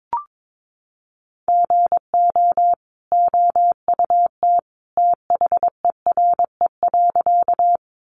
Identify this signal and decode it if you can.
Morse code, heard as a keyed tone, about 22 wpm, 700 Hz. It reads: GO OUT THERE.
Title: A Sound Of Thunder 22wpm